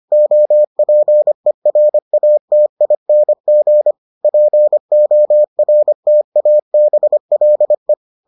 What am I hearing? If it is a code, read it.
Morse code, heard as a keyed tone, about 25 wpm, 600 Hz. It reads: OPERATING PORTABLE